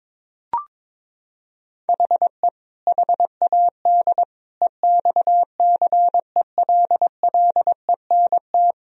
Morse code, heard as a keyed tone, about 22 wpm, 700 Hz. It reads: HE HAD EXCELLENT
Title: Sentences from Top 1000 Words - Review Part 3 - 22wpm